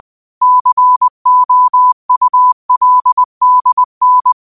Morse code, heard as a keyed tone, about 20 wpm, 1000 Hz. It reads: COULDN